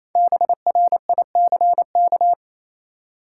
Morse code, heard as a keyed tone, about 28 wpm, 700 Hz. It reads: BRICK
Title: Five Letter Words - 28wpm